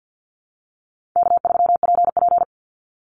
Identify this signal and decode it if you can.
Morse code, heard as a keyed tone, about 50 wpm, 700 Hz. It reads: K3PP
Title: Easy Call Signs Repeated 3x - 50wpm